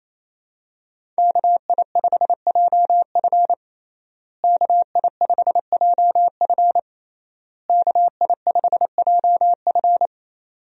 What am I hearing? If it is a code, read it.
Morse code, heard as a keyed tone, about 27 wpm, 700 Hz. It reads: KI5JF KI5JF KI5JF
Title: USA Call Signs - 5 Characters - Repeated 3x - 28wpm